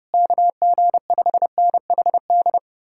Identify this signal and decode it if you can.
Morse code, heard as a keyed tone, about 33 wpm, 700 Hz. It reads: KG5NHD